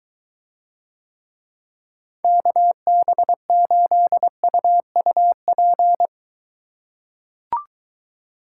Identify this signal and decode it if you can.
Morse code, heard as a keyed tone, about 21 wpm, 700 Hz. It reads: KB8UUP